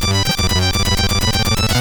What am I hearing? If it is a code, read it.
There is no code.